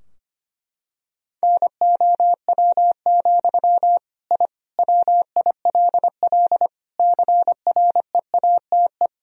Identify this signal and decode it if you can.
Morse code, heard as a keyed tone, about 25 wpm, 700 Hz. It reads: NOW, I WILL CREATE
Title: Sentences from Top 1000 Words - Review Part 3 - 25wpm